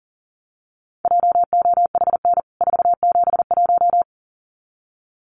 Morse code, heard as a keyed tone, about 40 wpm, 700 Hz. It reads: JOHN 471
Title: CWT Exchange 40wpm